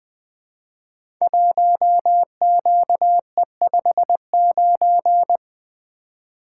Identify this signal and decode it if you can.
Morse code, heard as a keyed tone, about 20 wpm, 700 Hz. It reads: 1QE59
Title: Code Groups - 5 Letter:Numbers - 20wpm